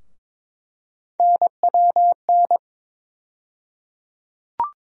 Morse code, heard as a keyed tone, about 22 wpm, 700 Hz. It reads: NWN
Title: USA Call Sign Postfixes - 3 Letters 22wpm